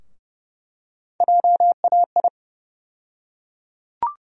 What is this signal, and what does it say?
Morse code, heard as a keyed tone, about 30 wpm, 700 Hz. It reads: JAI
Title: USA Call Sign Postfixes - 3 Letters 30wpm